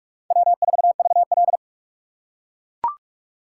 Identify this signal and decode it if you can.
Morse code, heard as a keyed tone, about 43 wpm, 700 Hz. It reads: W4VL
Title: USA Call Signs - 4x Character Spacing - 45wpm